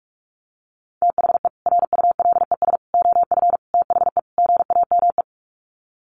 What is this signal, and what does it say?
Morse code, heard as a keyed tone, about 44 wpm, 700 Hz. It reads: THE RULES OF THE GAME
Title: Sets of 5 Words Encore2 45wpm